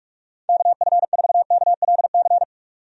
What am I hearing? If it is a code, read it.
Morse code, heard as a keyed tone, about 46 wpm, 700 Hz. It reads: KF4KLC
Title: USA Call Signs - 6 Characters - 45wpm